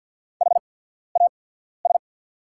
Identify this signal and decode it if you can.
Morse code, heard as a keyed tone, about 49 wpm, spaced as for 14 wpm, 690 Hz. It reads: HAS
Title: 8x Character Spacing - Top 100 Words - 50wpm